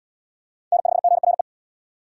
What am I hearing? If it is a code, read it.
Morse code, heard as a keyed tone, about 70 wpm, 700 Hz. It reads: N5GRE